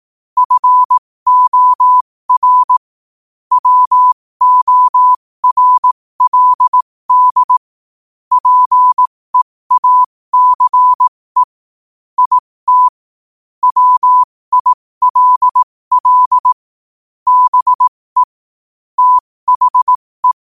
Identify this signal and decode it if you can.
Morse code, heard as a keyed tone, about 18 wpm, spaced as for 15 wpm, 1000 Hz. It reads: FOR WORLD PEACE IT WILL BE THE
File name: qotdfiles / 15wpm